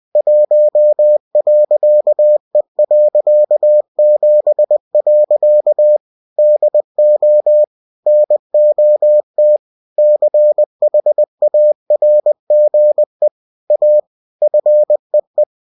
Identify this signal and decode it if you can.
Morse code, heard as a keyed tone, about 20 wpm, 600 Hz. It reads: 1.E.7. DO NOT CHARGE A FEE
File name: cw-books / winnie-the-pooh / winnie-20wpm-page-0116